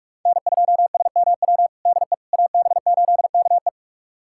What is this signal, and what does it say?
Morse code, heard as a keyed tone, about 43 wpm, 700 Hz. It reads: N2SMW DE AB8KE
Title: USA Callsign DE Callsign - 5 Characters - 45wpm